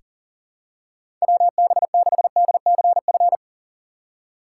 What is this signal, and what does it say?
Morse code, heard as a keyed tone, about 37 wpm, 700 Hz. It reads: WB6DCF
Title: USA Call Signs - 6 Characters - 2x Character Spacing - 40wpm